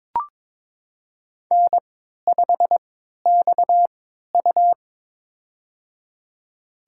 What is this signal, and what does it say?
Morse code, heard as a keyed tone, about 22 wpm, spaced as for 12 wpm, 700 Hz. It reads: N5XU
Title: USA Call Signs - 4 Characters - 3x Character Spacing - 22wpm